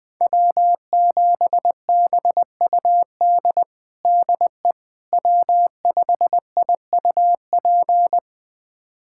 Morse code, heard as a keyed tone, about 20 wpm, 700 Hz. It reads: W7BUD DE W5IUP
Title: USA Callsign DE Callsign - 5 Characters - 20wpm